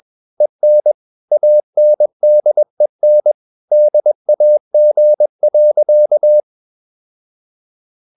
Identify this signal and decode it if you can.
Morse code, heard as a keyed tone, about 21 wpm, 600 Hz. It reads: EN ANDEN DAG.
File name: Ferie QRS 21wpm